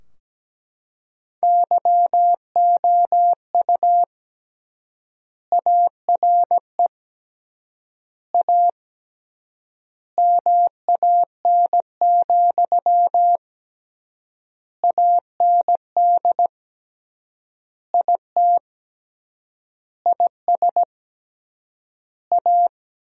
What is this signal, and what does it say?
Morse code, heard as a keyed tone, about 17 wpm, 700 Hz. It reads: YOU ARE A MAN, AND IT IS A
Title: 3x Word-Spacing - Sentences from Top 200 Words - 17wpm